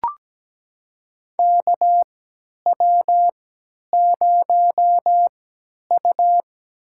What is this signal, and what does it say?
Morse code, heard as a keyed tone, about 17 wpm, spaced as for 10 wpm, 700 Hz. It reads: KW0U
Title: USA Call Signs - 3x Character Spacing - 17wpm